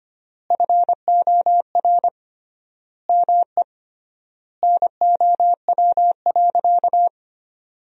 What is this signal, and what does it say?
Morse code, heard as a keyed tone, about 25 wpm, 700 Hz. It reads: FOR ME NOW.
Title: 3x Word-Spacing - Sentences from Top 200 Words - 25wpm